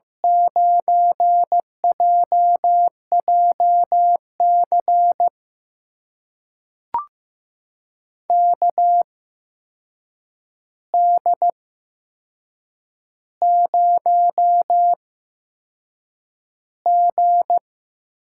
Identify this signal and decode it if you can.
Morse code, heard as a keyed tone, about 15 wpm, spaced as for 6 wpm, 700 Hz. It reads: 9JJC KD0G
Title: USA Call Signs - 8x Character Spacing - 15wpm